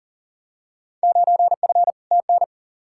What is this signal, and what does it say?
Morse code, heard as a keyed tone, about 38 wpm, 700 Hz. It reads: 9F TN
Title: Field Day Exchanges Encore - 40wpm